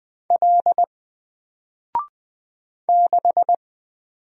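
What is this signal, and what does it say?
Morse code, heard as a keyed tone, about 20 wpm, 700 Hz. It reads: L 6
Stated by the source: Single letter and number Morse drill at 20 wpm, from N4LCD.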